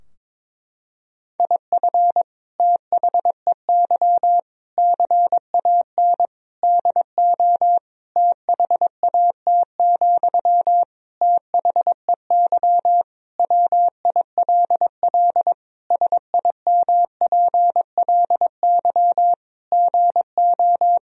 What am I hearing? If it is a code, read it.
Morse code, heard as a keyed tone, about 22 wpm, 700 Hz. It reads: IF THEY CAN DO THAT, THEY WILL SIMPLY GO
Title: Sentences from Top 1000 Words - Review Part 4 - 22wpm